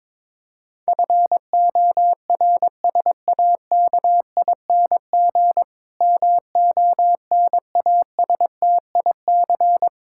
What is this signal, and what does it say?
Morse code, heard as a keyed tone, about 22 wpm, 700 Hz. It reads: FORSAKING MONASTIC
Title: Pangram Challenge v2 - 22wpm